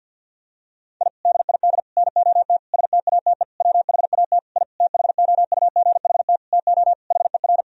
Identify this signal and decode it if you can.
Morse code, heard as a keyed tone, about 49 wpm, 700 Hz. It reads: I DID NOT STATE WHAT I THOUGHT TO HER
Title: Sentences from Top 1000 Words - Review Part 2 - 50wpm